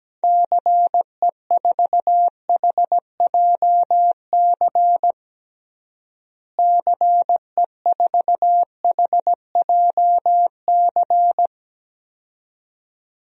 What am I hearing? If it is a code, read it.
Morse code, heard as a keyed tone, about 17 wpm, 700 Hz. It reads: CE4HJC CE4HJC
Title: DXCC-100 Call Signs - 6 Characters - Repeated 2x - 17wpm